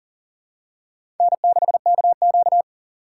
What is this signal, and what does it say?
Morse code, heard as a keyed tone, about 40 wpm, 700 Hz. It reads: N6KQ
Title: USA Call Signs - 4 Characters - 6x Character Spacing - 40wpm